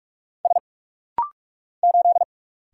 Morse code, heard as a keyed tone, about 45 wpm, 700 Hz. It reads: S 8
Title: Single Letter-Number Alt - Rapid-Fire 45wpm